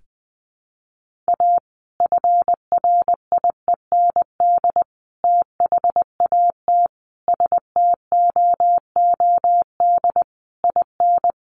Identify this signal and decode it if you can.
Morse code, heard as a keyed tone, about 20 wpm, 700 Hz. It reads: A FRIEND THAT STOOD IN
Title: Sentences from Top 1500 Words 20wpm